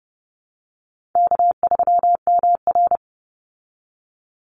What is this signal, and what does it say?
Morse code, heard as a keyed tone, about 30 wpm, 700 Hz. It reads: K3MR
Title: Call Signs Repeated 3x - 30wpm